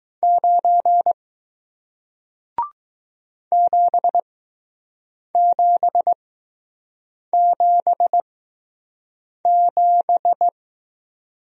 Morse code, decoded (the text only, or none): 9 7777